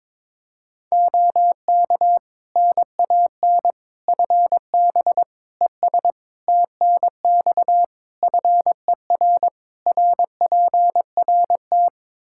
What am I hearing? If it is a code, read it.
Morse code, heard as a keyed tone, about 21 wpm, 700 Hz. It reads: OK NAN FB ES TNX FER RPRT
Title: Practice QSO - Exchange 2 - Info - 22wpm